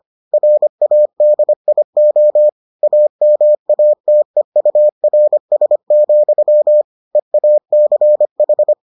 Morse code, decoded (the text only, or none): RADIO AMATEURS, EACH